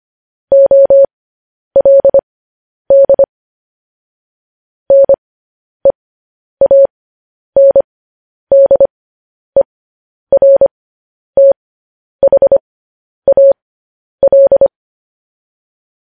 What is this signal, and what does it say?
Morse code, heard as a keyed tone, about 25 wpm, spaced as for 10 wpm, 550 Hz. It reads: OLD NEANDERTHAL